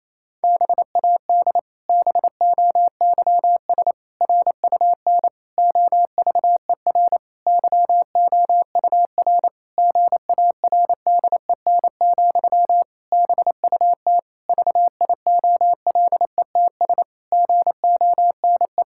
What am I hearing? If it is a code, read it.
Morse code, heard as a keyed tone, about 28 wpm, 700 Hz. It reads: BAD BOYS RUN OVER YOUR GARDEN, BUT VIOLETS GONE
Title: Pangram Challenge v2 - 28wpm